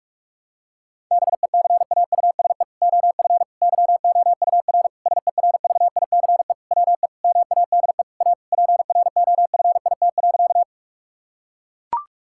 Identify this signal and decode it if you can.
Morse code, heard as a keyed tone, about 41 wpm, 700 Hz. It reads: BECAUSE OF YOUR SERVICE WE MADE A PROFIT.
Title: Sentences from Top 1000 Words - Review Part 5 - 45wpm